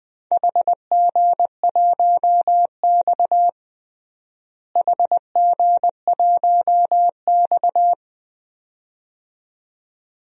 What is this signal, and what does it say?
Morse code, heard as a keyed tone, about 20 wpm, 700 Hz. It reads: HG1X HG1X
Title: DXCC-100 Call Signs - 4 Characters - Repeated 2x - 20wpm